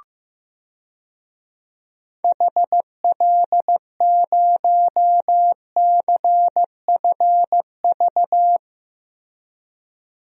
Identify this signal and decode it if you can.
Morse code, heard as a keyed tone, about 15 wpm, 700 Hz. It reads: HL0CFV
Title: DXCC-100 Call Signs - 6 Characters - Repeated 2x - 15wpm